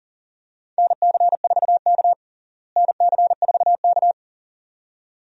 Morse code, heard as a keyed tone, about 37 wpm, 700 Hz. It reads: NC4K NC4K
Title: USA Call Signs - 4 Characters - Repeated 2x - 40wpm